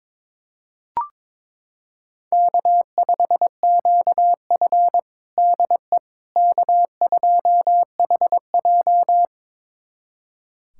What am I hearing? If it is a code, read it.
Morse code, heard as a keyed tone, about 22 wpm, 700 Hz. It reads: K5QF DE K2HJ
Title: USA Callsign DE Callsign - 4 Characters - 22wpm